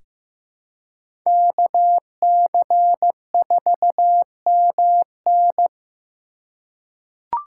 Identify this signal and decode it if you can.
Morse code, heard as a keyed tone, about 15 wpm, 700 Hz. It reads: KC4MN
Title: USA Call Signs - 5 Characters - 3x Character Spacing - 15wpm